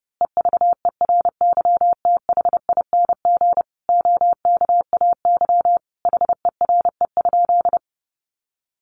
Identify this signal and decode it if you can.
Morse code, heard as a keyed tone, about 30 wpm, 700 Hz. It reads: EVERYTHING OKAY HERE?
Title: A Sound Of Thunder 30wpm